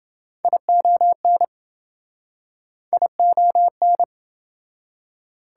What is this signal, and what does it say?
Morse code, heard as a keyed tone, about 27 wpm, 700 Hz. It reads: ION ION